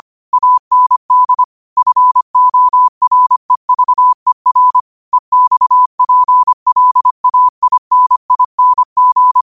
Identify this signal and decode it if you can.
Morse code, heard as a keyed tone, about 25 wpm, 1000 Hz. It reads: AND FOREVER EXPLAINING
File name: qotdfiles / 25wpm